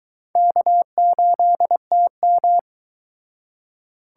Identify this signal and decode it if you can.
Morse code, heard as a keyed tone, about 23 wpm, 700 Hz. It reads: K8TM